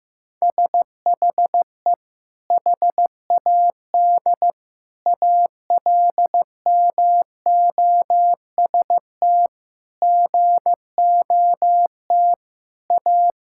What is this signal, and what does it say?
Morse code, heard as a keyed tone, about 16 wpm, 700 Hz. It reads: SHE HAD ALMOST GOT A
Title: Sentences from Top 1000 Words - Review Part 5 - 15wpm